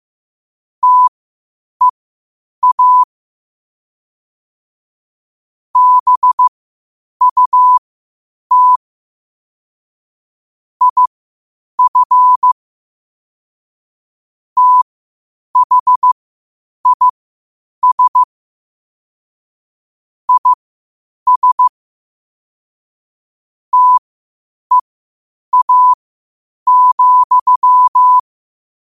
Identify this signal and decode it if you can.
Morse code, heard as a keyed tone, about 15 wpm, spaced as for 8 wpm, 1000 Hz. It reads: TEA BUT IF THIS IS TEA,
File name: qotdfiles / 8wpm